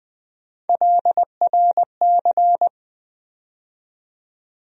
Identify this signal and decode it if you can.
Morse code, heard as a keyed tone, about 20 wpm, 700 Hz. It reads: LRC
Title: USA Call Sign Postfixes - 3 Letters 20wpm